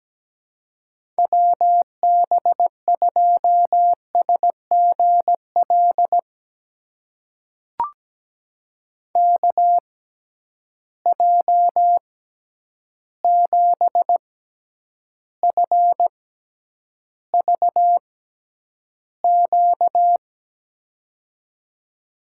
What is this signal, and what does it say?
Morse code, heard as a keyed tone, about 17 wpm, spaced as for 9 wpm, 700 Hz. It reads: WB2SGL KJ7FVQ
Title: USA Call Signs - 6 Characters - 6x Character Spacing - 17wpm